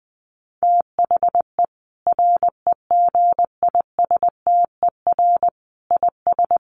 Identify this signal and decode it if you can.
Morse code, heard as a keyed tone, about 20 wpm, 700 Hz. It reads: THE REGISTER IS
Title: Sentences from Top 1200 Words 20wpm